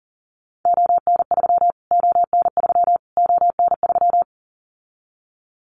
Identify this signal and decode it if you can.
Morse code, heard as a keyed tone, about 40 wpm, 700 Hz. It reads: ON3 ON3 ON3
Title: Callsign Prefixes for Top 25 DXCC - Repeated 3x - 40wpm